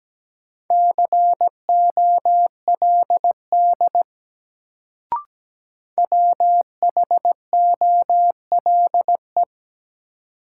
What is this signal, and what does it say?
Morse code, heard as a keyed tone, about 17 wpm, 700 Hz. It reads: COLD WHOLE
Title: Sentences from Top 400 Words - New Word Drill - 17wpm